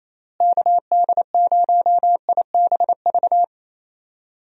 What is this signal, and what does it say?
Morse code, heard as a keyed tone, about 28 wpm, 700 Hz. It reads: KD0IBV
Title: USA Call Signs - 6 Characters - Repeated 2x - 28wpm